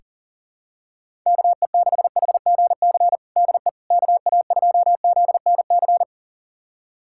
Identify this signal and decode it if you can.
Morse code, heard as a keyed tone, about 40 wpm, 700 Hz. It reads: KE6HGC DE KA2ZNC
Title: USA Callsign DE Callsign - 6 Characters - 40wpm